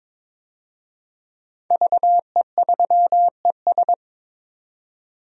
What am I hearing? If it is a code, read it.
Morse code, heard as a keyed tone, about 22 wpm, 700 Hz. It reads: VE3ES
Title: DXCC-100 Call Signs - 5 Characters - 22wpm